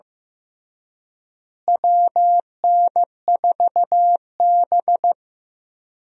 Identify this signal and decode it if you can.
Morse code, heard as a keyed tone, about 15 wpm, 700 Hz. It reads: WN4B
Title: USA Call Signs - 4 Characters - Repeated 3x - 15wpm